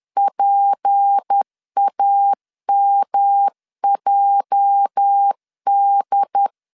Morse code, heard as a keyed tone, about 10 wpm, 800 Hz. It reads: PAMJD
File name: LEZIONI / 10 SIGNS / 10B INT 10wpm 2844